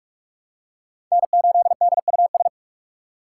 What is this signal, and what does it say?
Morse code, heard as a keyed tone, about 46 wpm, 700 Hz. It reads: N8DUS
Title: K1USN SST QSO Practice - 45wpm